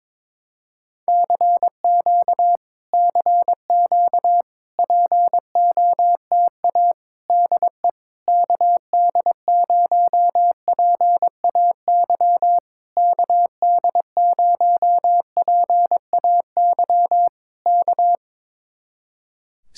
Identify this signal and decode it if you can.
Morse code, heard as a keyed tone, about 22 wpm, 700 Hz. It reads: CQ CQ POTA DE KD0PAY KD0PAY K
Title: Speed-Racing Common QSO Phrases Encore - 22wpm